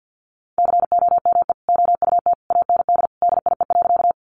Morse code, heard as a keyed tone, about 50 wpm, 700 Hz. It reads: COME OUT AND DIE.
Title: Sentences from Top 1300 Words 50wpm